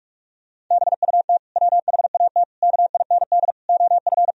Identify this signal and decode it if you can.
Morse code, heard as a keyed tone, about 46 wpm, 700 Hz. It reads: BUT WHAT KIND OF
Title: Sentences from Top 1000 Words - Review Part 1 - 45wpm